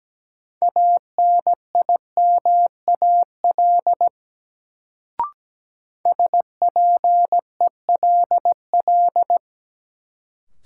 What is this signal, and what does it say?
Morse code, heard as a keyed tone, about 17 wpm, 700 Hz. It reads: ANIMAL SPELL
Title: Sentences from Top 1000 Words - Review Part 1 - New Word Drill - 17wpm